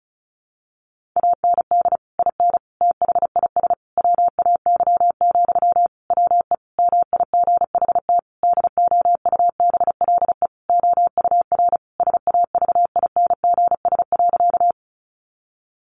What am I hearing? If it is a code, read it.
Morse code, heard as a keyed tone, about 34 wpm, 700 Hz. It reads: AND IN THIS WAY, WE MIGHT DOUBLE OUR SAVINGS.
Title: Sentences from Top 1400 Words 35wpm